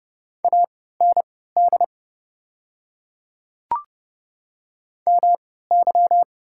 Morse code, heard as a keyed tone, about 30 wpm, spaced as for 17 wpm, 700 Hz. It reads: AND MY
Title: 3x Character Spacing - Top 100 Words - 30wpm